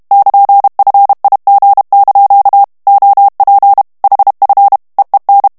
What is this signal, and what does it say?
Morse code, heard as a keyed tone, about 32 wpm, 800 Hz. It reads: <KN>FIG) OP HF EEN
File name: HOLLANDAIS texte clair 1 vitesse 30 mots minute